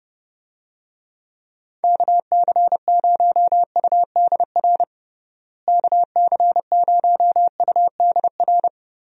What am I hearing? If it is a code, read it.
Morse code, heard as a keyed tone, about 30 wpm, 700 Hz. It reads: KC0UDR KC0UDR
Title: USA Call Signs - 6 Characters - Repeated 2x - 30wpm